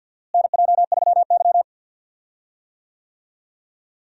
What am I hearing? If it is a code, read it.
Morse code, heard as a keyed tone, about 51 wpm, 700 Hz. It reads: NJ3Y
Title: USA Call Signs - 4 Characters - Repeated 3x - 50wpm